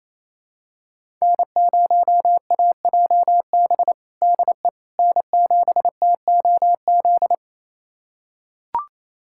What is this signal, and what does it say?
Morse code, heard as a keyed tone, about 28 wpm, 700 Hz. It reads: N0AJB DE N7TOZ
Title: USA Callsign DE Callsign - 5 Characters - 28wpm